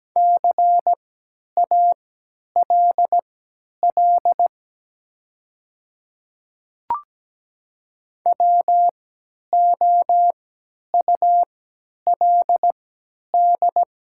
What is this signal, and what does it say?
Morse code, heard as a keyed tone, about 17 wpm, spaced as for 10 wpm, 700 Hz. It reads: CALL WOULD
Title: 3x Character Spacing - Top 100 Words - 17wpm